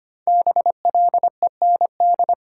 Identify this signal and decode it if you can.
Morse code, heard as a keyed tone, about 25 wpm, 700 Hz. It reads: BLEND
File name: 5 Letter Words - First 3 Letters Given - 25wpm